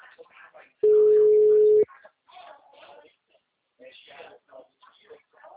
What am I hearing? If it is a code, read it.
Ringback tone (Europe, 425 Hz), with speech.